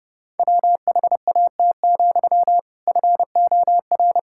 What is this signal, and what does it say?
Morse code, heard as a keyed tone, about 30 wpm, 700 Hz. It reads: WHAT, FOR
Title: Sentences from Top 1000 Words - Review Part 5 - 30wpm